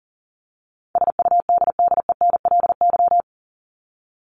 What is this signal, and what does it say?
Morse code, heard as a keyed tone, about 40 wpm, 700 Hz. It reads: SUDDENLY
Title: LY Suffix - Top 100 Words with Base Words 40wpm